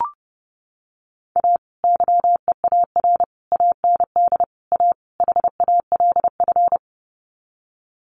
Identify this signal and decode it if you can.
Morse code, heard as a keyed tone, about 30 wpm, 700 Hz. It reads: A YEAR AND A HALF
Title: Sets of 5 Words 30wpm